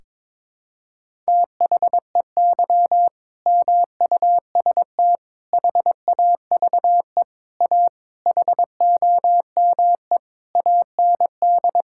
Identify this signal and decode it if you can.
Morse code, heard as a keyed tone, about 22 wpm, 700 Hz. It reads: THEY MUST HAVE A HOME AND
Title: Sentences from Top 1000 Words - Review Part 5 - 22wpm